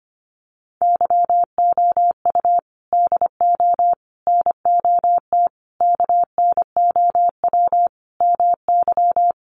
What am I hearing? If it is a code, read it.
Morse code, heard as a keyed tone, about 25 wpm, 700 Hz. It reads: YOU DO NOT KNOW MY
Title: Sentences from Top 600 Words 25wpm